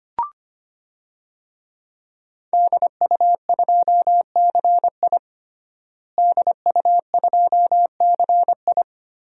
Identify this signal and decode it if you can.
Morse code, heard as a keyed tone, about 25 wpm, 700 Hz. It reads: DU2CI DU2CI
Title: DXCC-100 Call Signs - 5 Characters - Repeated 3x - 25wpm